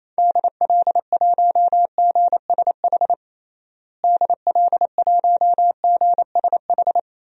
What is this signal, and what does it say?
Morse code, heard as a keyed tone, about 28 wpm, 700 Hz. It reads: DL1GSH DL1GSH
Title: DXCC-100 Call Signs - 6 Characters - Repeated 2x - 28wpm